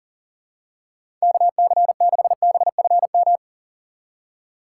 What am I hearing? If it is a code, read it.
Morse code, heard as a keyed tone, about 40 wpm, 700 Hz. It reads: KC6BFM
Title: USA Call Signs - 6 Characters - 4x Character Spacing - 40wpm